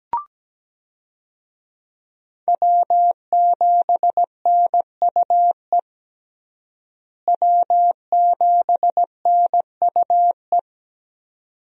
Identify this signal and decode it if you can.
Morse code, heard as a keyed tone, about 17 wpm, 700 Hz. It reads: W7NUE W7NUE
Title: USA Call Signs - 5 Characters - Repeated 2x - 17wpm